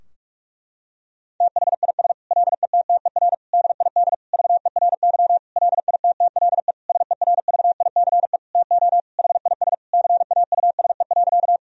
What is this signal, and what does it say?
Morse code, heard as a keyed tone, about 46 wpm, 700 Hz. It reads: THIS LETTER DID VERY LITTLE SERVICE TO HIS CAUSE.
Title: Sentences from Top 1000 Words - Review Part 4 - 45wpm